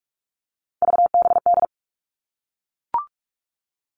Morse code, heard as a keyed tone, about 44 wpm, 700 Hz. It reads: VBD
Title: All Cut Numbers - 3-Digit 45wpm